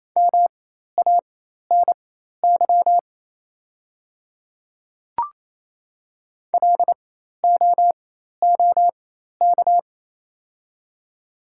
Morse code, heard as a keyed tone, about 28 wpm, spaced as for 13 wpm, 700 Hz. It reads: MANY LOOK
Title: 4x Character Spacing - Top 100 Words - 28wpm